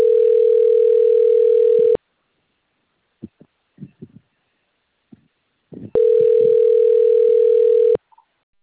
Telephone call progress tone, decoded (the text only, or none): ringback tone